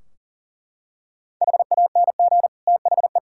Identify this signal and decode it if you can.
Morse code, heard as a keyed tone, about 37 wpm, 700 Hz. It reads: HANG THE
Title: Sentences from Top 1600 Words 40wpm